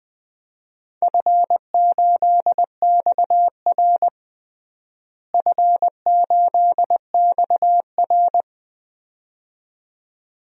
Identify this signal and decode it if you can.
Morse code, heard as a keyed tone, about 21 wpm, 700 Hz. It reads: F8XR F8XR
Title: DXCC-100 Call Signs - 4 Characters - Repeated 2x - 20wpm